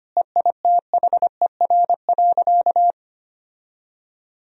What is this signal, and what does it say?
Morse code, heard as a keyed tone, about 25 wpm, 700 Hz. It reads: EITHER.
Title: Sentences from Top 1000 Words - Review Part 4 - 25wpm